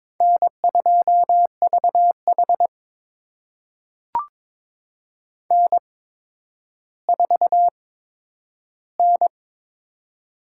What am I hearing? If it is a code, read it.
Morse code, heard as a keyed tone, about 22 wpm, spaced as for 11 wpm, 700 Hz. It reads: N2VH N4N
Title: USA Call Signs - 4 Characters - 8x Character Spacing - 22wpm